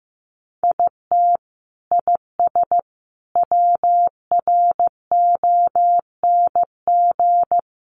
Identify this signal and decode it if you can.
Morse code, heard as a keyed tone, about 15 wpm, 700 Hz. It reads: IT IS WRONG
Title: Sentences from Top 1300 Words 15wpm